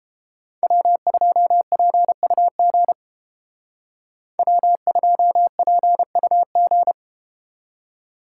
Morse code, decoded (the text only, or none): W2PUG W2PUG